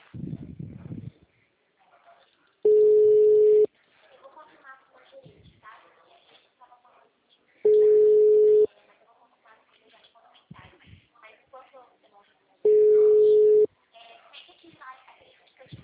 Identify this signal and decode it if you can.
Ringback tone (Europe, 425 Hz), with speech.
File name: out- -2002- - - .26